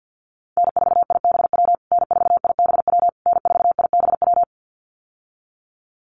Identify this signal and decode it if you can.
Morse code, heard as a keyed tone, about 50 wpm, 700 Hz. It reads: N4IBW N4IBW N4IBW
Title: USA Call Signs Repeated 3x 50wpm